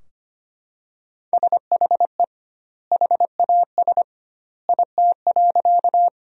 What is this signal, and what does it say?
Morse code, heard as a keyed tone, about 25 wpm, 700 Hz. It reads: SHE HAS IT.
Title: 2x Word-Spacing - Sentences from Top 100 Words - 25wpm